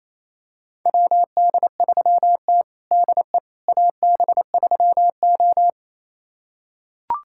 Morse code, heard as a keyed tone, about 28 wpm, 700 Hz. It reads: WD3T DE AB3O
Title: USA Callsign DE Callsign - 4 Characters - 28wpm